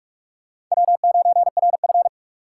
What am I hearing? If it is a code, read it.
Morse code, heard as a keyed tone, about 45 wpm, 700 Hz. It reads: W9RF
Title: USA Call Signs - 4 Characters - 4x Character Spacing - 45wpm